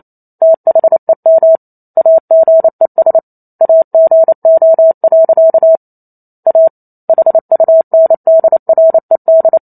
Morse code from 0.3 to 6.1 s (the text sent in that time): THEM AGES AGO.